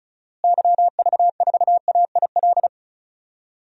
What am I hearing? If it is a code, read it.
Morse code, heard as a keyed tone, about 36 wpm, 700 Hz. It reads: YV4AIL
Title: DXCC-100 Call Signs - 6 Characters - Repeated 2x - 35wpm